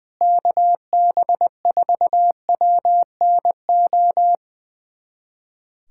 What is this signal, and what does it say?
Morse code, heard as a keyed tone, about 20 wpm, 700 Hz. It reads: KB4WNO
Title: USA Callsign DE Callsign 20wpm